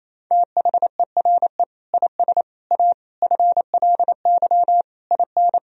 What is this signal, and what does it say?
Morse code, heard as a keyed tone, about 28 wpm, 700 Hz. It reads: THERE IS A FLY IN
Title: Sentences from Top 1000 Words - Review Part 4 - 28wpm